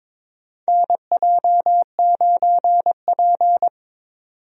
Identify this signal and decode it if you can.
Morse code, heard as a keyed tone, about 22 wpm, 700 Hz. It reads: NJ9P
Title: USA Call Signs - 4 Characters - 22wpm